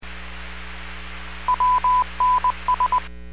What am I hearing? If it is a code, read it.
Morse code, heard as a keyed tone, about 20 wpm, 1000 Hz. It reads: WNS